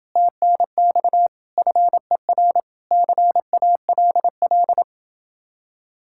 Morse code, heard as a keyed tone, about 29 wpm, 700 Hz. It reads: TNX FER CALL